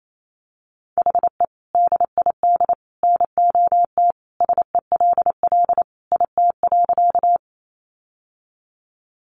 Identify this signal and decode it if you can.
Morse code, heard as a keyed tone, about 28 wpm, 700 Hz. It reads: HE DID NOT SELL IT.
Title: Sentences from Top 1100 Words 28wpm